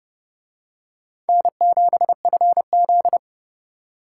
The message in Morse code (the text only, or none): N7FZ